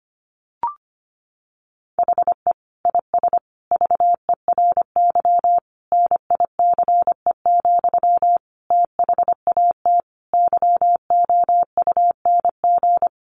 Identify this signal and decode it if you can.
Morse code, heard as a keyed tone, about 25 wpm, 700 Hz. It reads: HE IS VERY NICE, THAT YOUNG
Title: Sentences from Top 1100 Words 25wpm